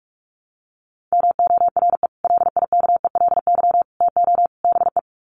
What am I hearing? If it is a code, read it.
Morse code, heard as a keyed tone, about 44 wpm, 700 Hz. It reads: MORE LIKELY TO BE
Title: Sets of 4 Words 45wpm